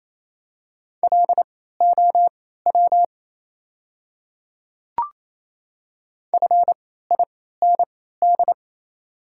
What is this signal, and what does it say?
Morse code, heard as a keyed tone, about 27 wpm, spaced as for 16 wpm, 690 Hz. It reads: LOW FIND
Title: 3x Character Spacing - Top 200 Words - 28wpm